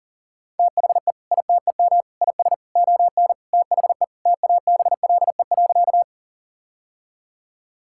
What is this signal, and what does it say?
Morse code, heard as a keyed tone, about 40 wpm, 700 Hz. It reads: THE ITEM IS ON THE TABLE.
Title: Sentences from Top 1000 Words - Review Part 6 - 40wpm